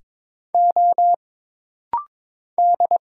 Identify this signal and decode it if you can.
Morse code, heard as a keyed tone, about 22 wpm, 700 Hz. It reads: O D
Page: This rapid-fire timing is geared to push your ICR (Instant Character Recognition) skills.